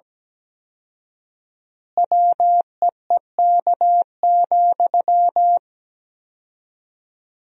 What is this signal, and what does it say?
Morse code, heard as a keyed tone, about 17 wpm, 700 Hz. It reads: WEEK,
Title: 4x Word-Spacing - Sentences from Top 500 Words - 17wpm